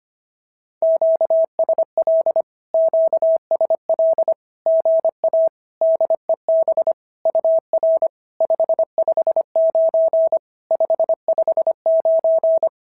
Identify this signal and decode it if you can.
Morse code, heard as a keyed tone, about 25 wpm, 690 Hz. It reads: QSL QSL GA DEB UR 559 559
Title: SOTA QSO Practice 25wpm